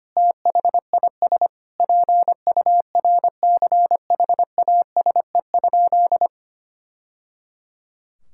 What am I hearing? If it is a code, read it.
Morse code, heard as a keyed tone, about 25 wpm, 700 Hz. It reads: THIS PURCHASE?
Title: Sentences from Top 1000 Words - Review Part 5 - 25wpm